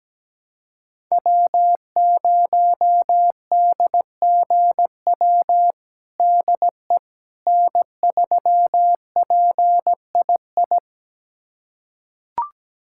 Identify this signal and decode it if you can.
Morse code, heard as a keyed tone, about 17 wpm, 700 Hz. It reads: W0DGW DE N3PII
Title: USA Callsign DE Callsign - 5 Characters - 17wpm